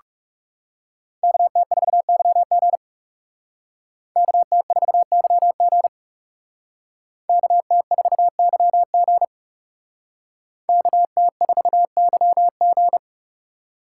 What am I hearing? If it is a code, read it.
Morse code, heard as a keyed tone, about 38 wpm, 700 Hz. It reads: KT4YG KT4YG KT4YG KT4YG